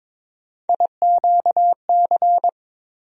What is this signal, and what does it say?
Morse code, heard as a keyed tone, about 22 wpm, 700 Hz. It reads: IQC
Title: USA Call Sign Postfixes - 3 Letters 22wpm